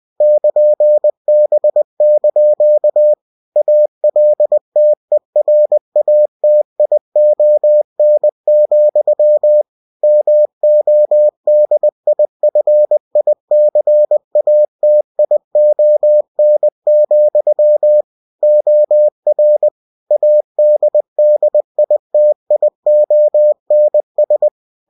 Morse code, heard as a keyed tone, about 20 wpm, 600 Hz. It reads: <KN>B) ALTERATION, MODIFICATION, OR ADDITIONS
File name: cw-books / winnie-the-pooh / winnie-20wpm-page-0120